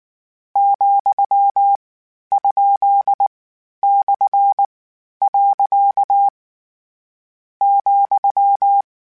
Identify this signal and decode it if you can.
Morse code, heard as a keyed tone, about 19 wpm, 800 Hz. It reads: , ? / . ,
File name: Misc / Misc-mp3s / Punct @18